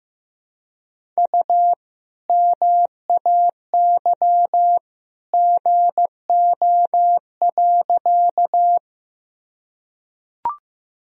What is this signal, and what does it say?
Morse code, heard as a keyed tone, about 15 wpm, 700 Hz. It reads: U MAY GO.
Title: Sentences from Top 100 Words and Abbreviations 15wpm